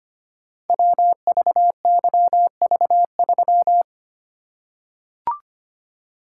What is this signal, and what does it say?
Morse code, heard as a keyed tone, about 25 wpm, 700 Hz. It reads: WVYV3
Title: Code Groups - 5 Letter:Numbers - 25wpm